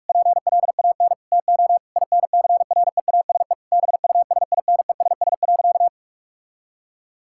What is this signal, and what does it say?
Morse code, heard as a keyed tone, about 46 wpm, 700 Hz. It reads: PLAN TO INCREASE BUSINESS.
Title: Sentences from Top 1000 Words - Review Part 6 - 45wpm